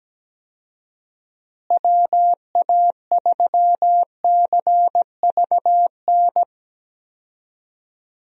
Morse code, heard as a keyed tone, about 17 wpm, 700 Hz. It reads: WA3CVN
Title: USA Call Signs - 6 Characters - Repeated 3x - 17wpm